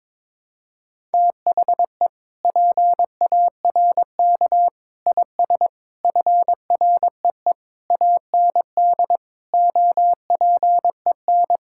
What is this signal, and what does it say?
Morse code, heard as a keyed tone, about 22 wpm, 700 Hz. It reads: THE PARK IS FREE AND OPEN
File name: Sentences with 7-Word Phrases - 22wpm